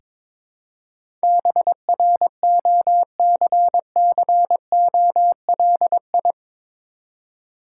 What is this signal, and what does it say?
Morse code, heard as a keyed tone, about 23 wpm, 690 Hz. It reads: BROCCOLI
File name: Difficult Words with Hints - 22wpm